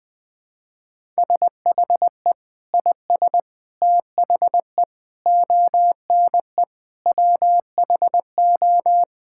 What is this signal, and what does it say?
Morse code, heard as a keyed tone, about 20 wpm, 700 Hz. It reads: SHE IS THE ONE WHO
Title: Sentences with 2-Word Phrases - Spoken Beginning - 20wpm